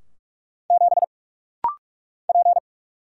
Morse code, heard as a keyed tone, about 44 wpm, 700 Hz. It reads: 7 P
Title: Single Letter-Number Alt - Rapid-Fire 45wpm